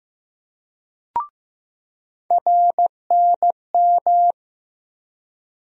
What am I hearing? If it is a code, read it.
Morse code, heard as a keyed tone, about 15 wpm, 700 Hz. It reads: RNM
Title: USA Call Sign Postfixes - 3 Letters 15wpm